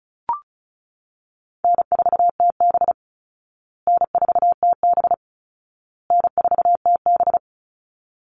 Morse code, heard as a keyed tone, about 34 wpm, 700 Hz. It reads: N4TB N4TB N4TB
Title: Call Signs Repeated 2x - 35wpm